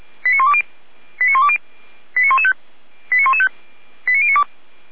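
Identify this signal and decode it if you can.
Five-tone selective call (ZVEI), heard as quick runs of five tones, tones of about 70 ms. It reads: 89120 89120 89115 89115 89092
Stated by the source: ZVEI five-tone selective call 89020 received on SDR.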